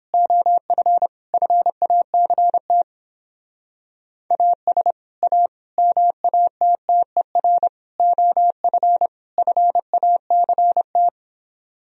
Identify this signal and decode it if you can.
Morse code, heard as a keyed tone, about 31 wpm, 700 Hz. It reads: OF FACT AS A MATTER OF FACT